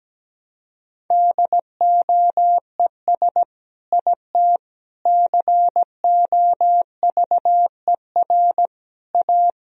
Morse code, heard as a keyed tone, about 17 wpm, 700 Hz. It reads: DOES IT COVER A
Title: Sentences from Top 1000 Words - Review Part 2 - 17wpm